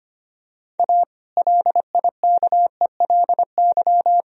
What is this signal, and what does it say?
Morse code, heard as a keyed tone, about 25 wpm, 700 Hz. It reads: A LIKELY
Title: Sentences from Top 1000 Words - Review Part 4 - 25wpm